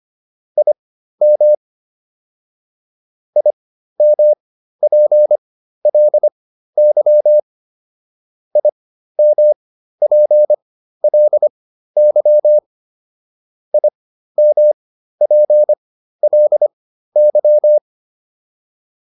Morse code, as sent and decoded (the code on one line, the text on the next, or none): .. -- / .. -- .--. .-.. -.-- / .. -- .--. .-.. -.-- / .. -- .--. .-.. -.--
IM IMPLY IMPLY IMPLY